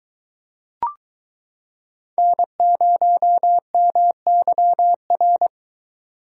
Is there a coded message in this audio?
Yes: Morse code from 2.2 to 5.5 s, heard as a keyed tone, about 23 wpm, 700 Hz. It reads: N0MYR